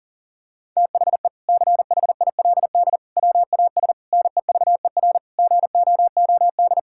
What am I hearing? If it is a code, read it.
Morse code, heard as a keyed tone, about 37 wpm, 700 Hz. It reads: THE CHILD WAS NEVER GOOD
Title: Sentences from Top 1000 Words - Review Part 6 - 40wpm